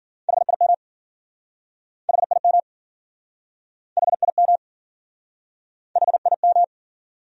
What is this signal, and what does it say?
Morse code, heard as a keyed tone, about 54 wpm, 700 Hz. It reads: HIM HIM HIM HIM